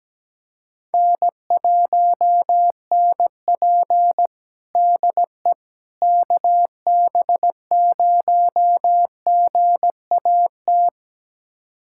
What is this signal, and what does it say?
Morse code, heard as a keyed tone, about 17 wpm, 700 Hz. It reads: N1NP DE KB0GAT
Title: USA Callsign DE Callsign 17wpm